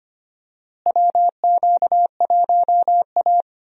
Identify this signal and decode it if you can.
Morse code, heard as a keyed tone, about 25 wpm, 700 Hz. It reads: WQ1A
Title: USA Call Signs - 4 Characters - Repeated 2x - 25wpm